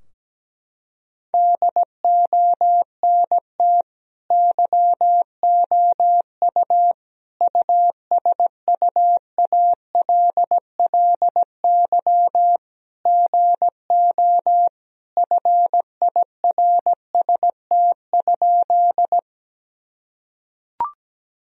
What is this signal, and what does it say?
Morse code, heard as a keyed tone, about 17 wpm, 700 Hz. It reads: DONT YOU USUALLY GO FIRST?
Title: Sentences from Top 1000 Words - Review Part 3 - 17wpm